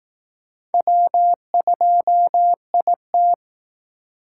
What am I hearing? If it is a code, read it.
Morse code, heard as a keyed tone, about 18 wpm, 700 Hz. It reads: W2IT
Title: Speed Racing - USA 1x2 Call Signs 15wpm